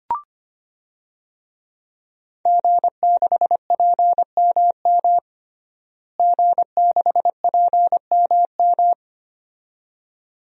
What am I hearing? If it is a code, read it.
Morse code, heard as a keyed tone, about 25 wpm, 700 Hz. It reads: G6PMM G6PMM
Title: DXCC-100 Call Signs - 5 Characters - Repeated 2x - 25wpm